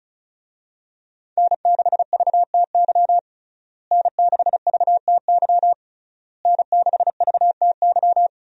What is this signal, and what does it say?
Morse code, heard as a keyed tone, about 34 wpm, 700 Hz. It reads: N6VTY N6VTY N6VTY
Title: USA Call Signs - 5 Characters - Repeated 3x - 35wpm